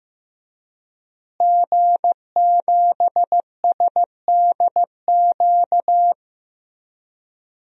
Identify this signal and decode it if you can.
Morse code, heard as a keyed tone, about 15 wpm, 700 Hz. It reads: G7SDQ
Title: DXCC-100 Call Signs - 5 Characters - Repeated 3x - 15wpm